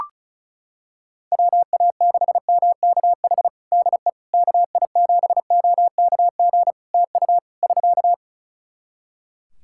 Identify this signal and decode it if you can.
Morse code, heard as a keyed tone, about 35 wpm, 700 Hz. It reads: WA6MKH DE KI7OKG TU <SK>
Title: Practice QSO - Exchange 4 - Ending - 35wpm